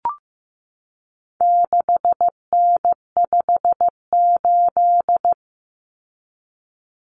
Morse code, heard as a keyed tone, about 15 wpm, 700 Hz. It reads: 6N58
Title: Common Cut Numbers - 4-Digit 15wpm